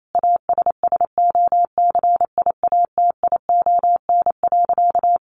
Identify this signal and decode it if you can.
Morse code, heard as a keyed tone, about 28 wpm, 700 Hz. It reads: ASSOCIATION.
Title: Sentences from Top 1200 Words 28wpm